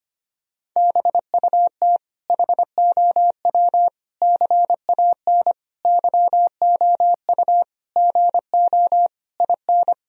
Morse code, heard as a keyed tone, about 25 wpm, 700 Hz. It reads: BUT HOW CAN YOU GO IN
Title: Sentences from Top 1000 Words - Review Part 2 - 25wpm